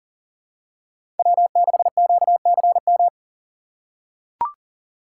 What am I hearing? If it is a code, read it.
Morse code, heard as a keyed tone, about 40 wpm, 700 Hz. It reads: W6QCM
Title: USA Call Signs - 5 Characters - 2x Character Spacing - 40wpm